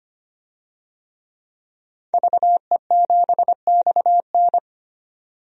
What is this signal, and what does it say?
Morse code, heard as a keyed tone, about 25 wpm, 700 Hz. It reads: VE7XN
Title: DXCC-100 Call Signs - 5 Characters - 25wpm